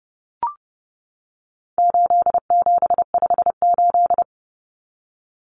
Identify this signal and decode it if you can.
Morse code, heard as a keyed tone, about 30 wpm, 700 Hz. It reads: 8758
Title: Numbers - 4-Digit 30wpm